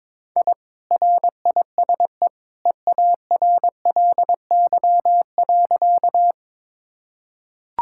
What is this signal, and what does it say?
Morse code, heard as a keyed tone, about 22 wpm, 700 Hz. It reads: I RISE EARLY.
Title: Sentences from Top 1000 Words - Review Part 6 - 22wpm